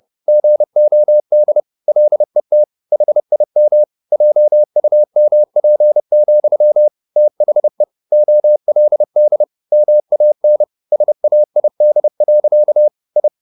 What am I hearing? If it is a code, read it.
Morse code, heard as a keyed tone, about 30 wpm, 600 Hz. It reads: GOD LET HIM JUMP, THE OLD MAN SAID. I
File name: the old man and the sea / the old man and the sea 1 30wpm